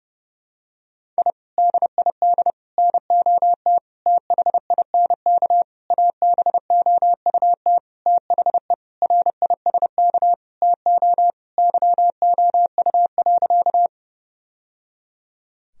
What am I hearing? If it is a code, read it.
Morse code, heard as a keyed tone, about 32 wpm, 700 Hz. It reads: I DID NOT THINK ABOUT THE RISK TO YOU.
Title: Sentences from Top 1000 Words - Review Part 4 - 30wpm